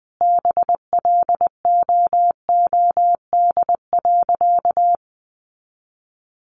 Morse code, heard as a keyed tone, about 20 wpm, 700 Hz. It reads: BLOOD.
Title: Sentences from Top 1200 Words 20wpm